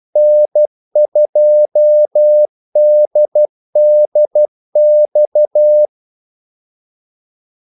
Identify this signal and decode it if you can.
Morse code, heard as a keyed tone, about 12 wpm, 600 Hz. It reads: N2DDX
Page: 2 ekstra ordmellemrum mellem kaldesignalerne